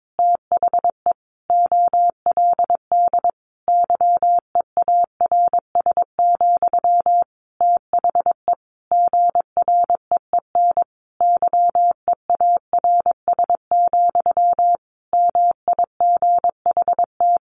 Morse code, heard as a keyed tone, about 22 wpm, 700 Hz. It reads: THE OLD YEARS, THE GREEN YEARS, MIGHT
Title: A Sound Of Thunder 22wpm